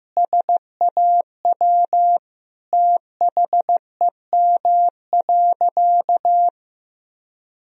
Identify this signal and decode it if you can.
Morse code, heard as a keyed tone, about 15 wpm, 700 Hz. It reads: SAW THEM.
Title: Sentences from Top 1000 Words - Review Part 1 - 15wpm